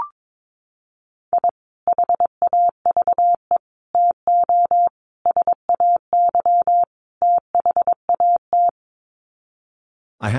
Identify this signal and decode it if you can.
Morse code, heard as a keyed tone, about 22 wpm, 700 Hz. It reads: I HAVE TO SAY THAT
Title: Sets of 5 Words Encore 22wpm